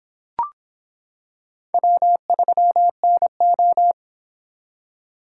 Morse code, heard as a keyed tone, about 26 wpm, 700 Hz. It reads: W3NO